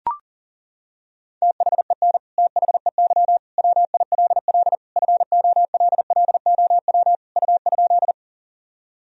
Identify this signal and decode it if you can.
Morse code, heard as a keyed tone, about 40 wpm, 700 Hz. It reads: THEN THEY WILL FOLLOW U?
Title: Sentences from Top 200 Words and Abbreviations 40wpm